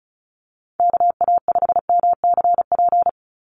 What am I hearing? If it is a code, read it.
Morse code, heard as a keyed tone, about 34 wpm, 700 Hz. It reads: KA5MCP
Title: USA Call Signs 35wpm